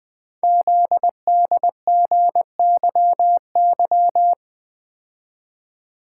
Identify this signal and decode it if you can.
Morse code, heard as a keyed tone, about 20 wpm, 700 Hz. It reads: ZDGYY
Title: Code Groups - 5 Letters - 20wpm